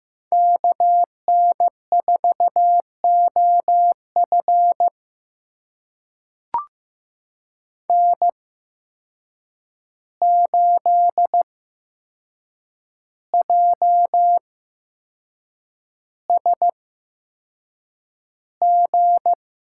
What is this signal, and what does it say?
Morse code, heard as a keyed tone, about 15 wpm, spaced as for 6 wpm, 700 Hz. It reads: KN4OF N8JSG
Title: USA Call Signs - 5 Characters - 8x Character Spacing - 15wpm